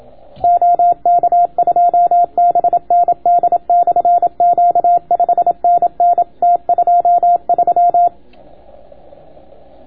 Morse code, heard as a keyed tone, about 27 wpm, 700 Hz. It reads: OK2BND/Q 5NN T23